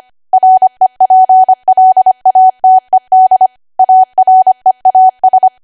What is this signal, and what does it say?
Morse code, heard as a keyed tone, about 25 wpm, 750 Hz. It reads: REPLATED AREAS
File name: Archive / 25 WPM / 25WPM